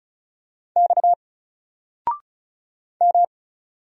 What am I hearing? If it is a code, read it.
Morse code, heard as a keyed tone, about 36 wpm, 700 Hz. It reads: X M
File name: Single-Letter-Number-35wpm